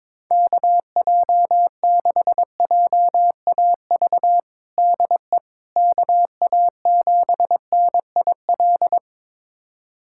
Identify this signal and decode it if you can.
Morse code, heard as a keyed tone, about 21 wpm, 700 Hz. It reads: KJ6JAV DE KA7NIL
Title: USA Callsign DE Callsign 22wpm